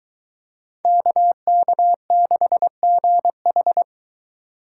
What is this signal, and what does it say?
Morse code, heard as a keyed tone, about 23 wpm, 700 Hz. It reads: KK6GH